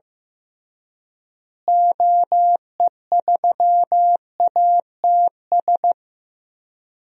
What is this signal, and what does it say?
Morse code, heard as a keyed tone, about 15 wpm, 700 Hz. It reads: OE3ATS
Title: DXCC-100 Call Signs - 6 Characters - Repeated 3x - 15wpm